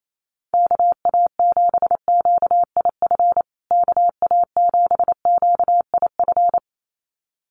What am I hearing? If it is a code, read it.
Morse code, heard as a keyed tone, about 28 wpm, 700 Hz. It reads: KA7QIF KA7QIF
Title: USA Call Signs Repeated 2x 28wpm